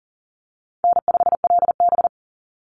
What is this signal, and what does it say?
Morse code, heard as a keyed tone, about 40 wpm, 700 Hz. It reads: N5LB
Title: Call Signs Repeated 3x - 40wpm